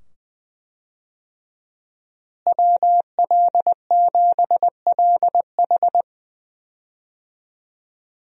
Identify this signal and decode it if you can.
Morse code, heard as a keyed tone, about 20 wpm, 700 Hz. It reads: WL7LH
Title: USA Call Signs - 5 Characters - Repeated 2x - 20wpm